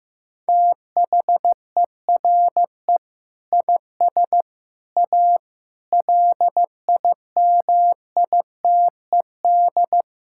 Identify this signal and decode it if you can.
Morse code, heard as a keyed tone, about 16 wpm, 700 Hz. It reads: THERE IS A LIMITED
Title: Sentences from Top 1000 Words - Review Part 6 - 15wpm